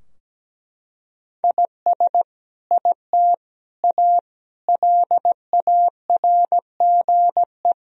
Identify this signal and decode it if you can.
Morse code, heard as a keyed tone, about 17 wpm, 700 Hz. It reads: IS IT A LARGE
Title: Sentences from Top 1000 Words - Review Part 4 - 17wpm